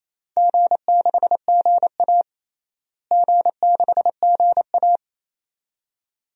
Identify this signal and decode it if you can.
Morse code, heard as a keyed tone, about 28 wpm, 700 Hz. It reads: G6GA G6GA
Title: DXCC-100 Call Signs - 4 Characters - Repeated 2x - 28wpm